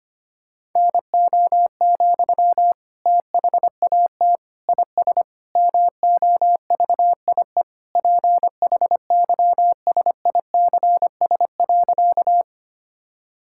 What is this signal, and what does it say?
Morse code, heard as a keyed tone, about 25 wpm, 700 Hz. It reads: NO, THAT IS MOVIE PHYSICS.
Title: Sentences from Top 1000 Words - Review Part 5 - 25wpm